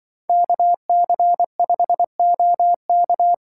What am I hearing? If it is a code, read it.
Morse code, heard as a keyed tone, about 24 wpm, 700 Hz. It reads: KC5OK
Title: Speed Racing - USA 2x2 Call Signs 20wpm